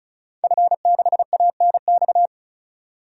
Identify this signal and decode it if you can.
Morse code, heard as a keyed tone, about 36 wpm, 700 Hz. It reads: F6ANX
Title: DXCC-100 Call Signs - 5 Characters - 35wpm